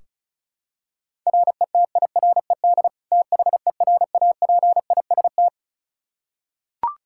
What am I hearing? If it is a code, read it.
Morse code, heard as a keyed tone, about 34 wpm, 700 Hz. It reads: RETIRED THERAPIST
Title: Practice QSO - Exchange 3 - More Info - 35wpm